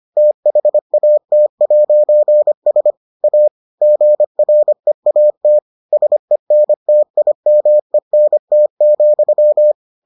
Morse code, heard as a keyed tone, about 25 wpm, 600 Hz. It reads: THAT'S A GREAT SENTIMENT,